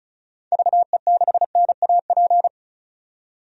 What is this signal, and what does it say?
Morse code, heard as a keyed tone, about 36 wpm, 700 Hz. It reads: VE6NAP
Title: DXCC-100 Call Signs - 6 Characters - 35wpm